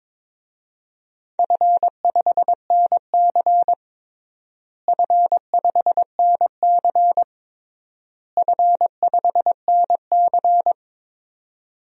Morse code, heard as a keyed tone, about 22 wpm, 700 Hz. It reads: F5NC F5NC F5NC
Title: DXCC-100 Call Signs - 4 Characters - Repeated 3x - 22wpm